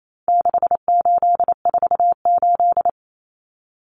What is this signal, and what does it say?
Morse code, heard as a keyed tone, about 28 wpm, 700 Hz. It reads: 6848
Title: Numbers - 4-Digit 28wpm